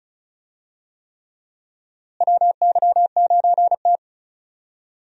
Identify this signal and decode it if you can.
Morse code, heard as a keyed tone, about 34 wpm, 700 Hz. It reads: WY9T
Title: USA Call Signs - 4 Characters - Repeated 3x - 35wpm